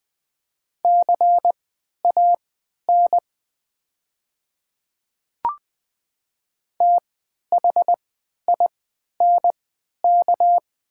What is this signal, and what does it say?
Morse code, heard as a keyed tone, about 21 wpm, spaced as for 12 wpm, 700 Hz. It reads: CAN THINK
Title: 3x Character Spacing - Top 200 Words - 20wpm